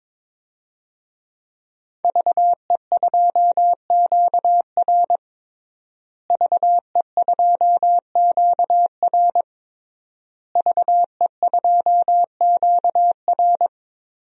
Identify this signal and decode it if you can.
Morse code, heard as a keyed tone, about 22 wpm, 700 Hz. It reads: VE2QR VE2QR VE2QR
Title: DXCC-100 Call Signs - 5 Characters - Repeated 3x - 22wpm